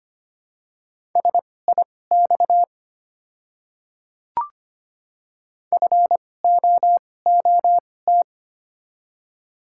Morse code, heard as a keyed tone, about 25 wpm, spaced as for 18 wpm, 700 Hz. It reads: SIX FOOT
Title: 2x Character Spacing - Top 500 Words - 25wpm